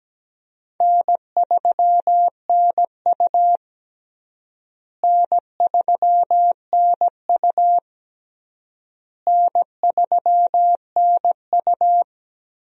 Morse code, heard as a keyed tone, about 17 wpm, 700 Hz. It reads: N3NU N3NU N3NU
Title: USA Call Signs - 4 Characters - Repeated 3x - 17wpm